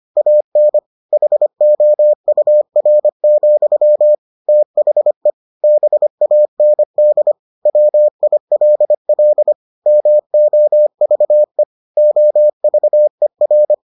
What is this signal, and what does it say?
Morse code, heard as a keyed tone, about 25 wpm, 600 Hz. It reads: AN HOUR, THE BAND WILL MOVE OVER